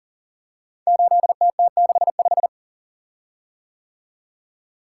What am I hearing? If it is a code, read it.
Morse code, heard as a keyed tone, about 40 wpm, 700 Hz. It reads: 8TT65
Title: Code Groups - 5 Letter:Numbers - 40wpm